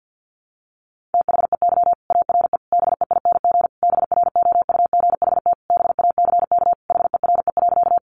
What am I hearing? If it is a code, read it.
Morse code, heard as a keyed tone, about 50 wpm, 700 Hz. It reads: THEY ARE BEING BROUGHT BACK HERE.
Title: Sentences from Top 600 Words 50wpm